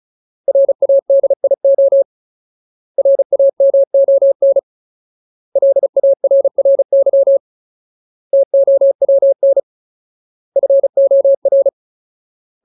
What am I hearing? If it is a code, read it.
Morse code, heard as a keyed tone, about 34 wpm, 550 Hz. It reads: RADIO RAMON LARRY TOWN FOR
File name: Ultimate Word List / 35WPM Ultimate